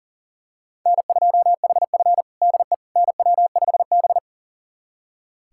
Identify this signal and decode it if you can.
Morse code, heard as a keyed tone, about 40 wpm, 700 Hz. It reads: N2HF DE NW5B
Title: USA Callsign DE Callsign - 4 Characters - 40wpm